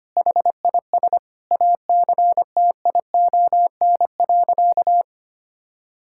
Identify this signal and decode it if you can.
Morse code, heard as a keyed tone, about 25 wpm, 700 Hz. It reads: HIS ACTION.
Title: Sentences from Top 1000 Words - Review Part 5 - 25wpm